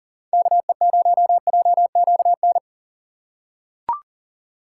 Morse code, heard as a keyed tone, about 40 wpm, 700 Hz. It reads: KE0JQN
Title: USA Call Signs - 6 Characters - 4x Character Spacing - 40wpm